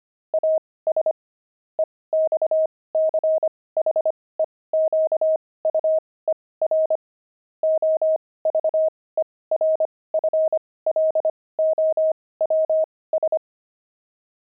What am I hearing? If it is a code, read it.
Morse code, heard as a keyed tone, about 25 wpm, spaced as for 18 wpm, 630 Hz. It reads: AS EXCHEQUER OVERFLOWS
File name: pangram-fs-18-cs-25-kibitzer